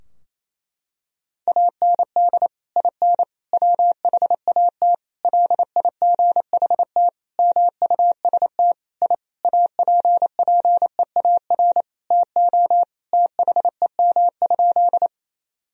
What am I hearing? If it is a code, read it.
Morse code, heard as a keyed tone, about 28 wpm, 700 Hz. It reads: AND IN WHAT LIGHT MUST I APPEAR TO THEM?
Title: Sentences from Top 1000 Words - Review Part 2 - 28wpm